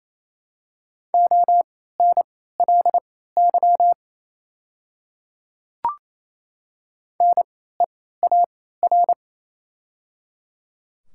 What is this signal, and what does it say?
Morse code, heard as a keyed tone, about 28 wpm, spaced as for 16 wpm, 700 Hz. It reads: ONLY NEAR
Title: 3x Character Spacing - Top 200 Words - 28wpm